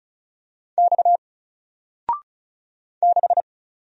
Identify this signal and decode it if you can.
Morse code, heard as a keyed tone, about 36 wpm, 700 Hz. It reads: X 6
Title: Single Letter-Number Alt 35wpm